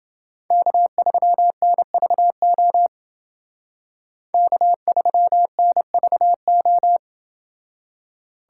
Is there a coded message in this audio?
Yes: Morse code from 0.5 to 7.0 s, heard as a keyed tone, about 29 wpm, 700 Hz. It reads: K3NVO K3NVO